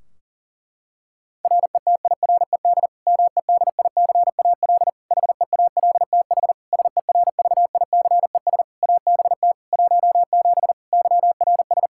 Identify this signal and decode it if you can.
Morse code, heard as a keyed tone, about 40 wpm, 700 Hz. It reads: RETIRED MEDICAL HEALTH SERVICES ABT 17 YRS
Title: K1USN SST QSO Practice 40wpm